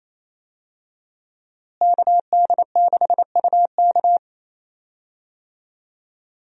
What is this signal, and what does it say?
Morse code, heard as a keyed tone, about 28 wpm, 700 Hz. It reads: KD6UK
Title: USA Call Signs - 5 Characters - Repeated 2x - 28wpm